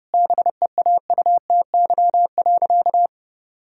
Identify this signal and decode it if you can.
Morse code, heard as a keyed tone, about 30 wpm, 700 Hz. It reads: BEAUTY.
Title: Sentences from Top 1000 Words - Review Part 2 - 30wpm